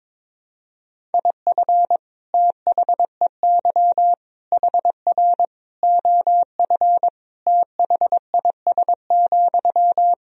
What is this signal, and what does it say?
Morse code, heard as a keyed tone, about 22 wpm, 700 Hz. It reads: IF THEY HR OF THIS,
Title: Sentences from Top 400 Words and Abbreviations 22wpm